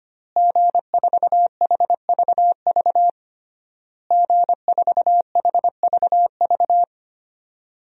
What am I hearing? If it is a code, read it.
Morse code, heard as a keyed tone, about 25 wpm, 700 Hz. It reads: G4HVV G4HVV
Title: DXCC-100 Call Signs - 5 Characters - Repeated 3x - 25wpm